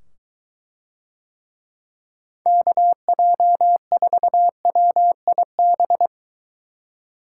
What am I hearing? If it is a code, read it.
Morse code, heard as a keyed tone, about 23 wpm, 700 Hz. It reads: KJ4WIB